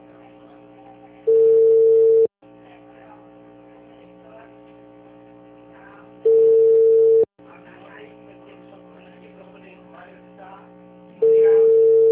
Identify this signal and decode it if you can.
Ringback tone (China, 450 Hz).
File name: out- -2003- - - .21881